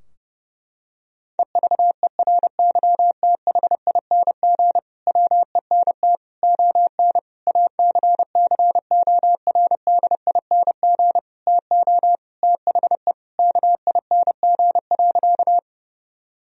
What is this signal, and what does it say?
Morse code, heard as a keyed tone, about 30 wpm, 700 Hz. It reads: EVERYTHING WENT ON ACCORDING TO THE KING.
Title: Sentences from Top 1000 Words - Review Part 4 - 30wpm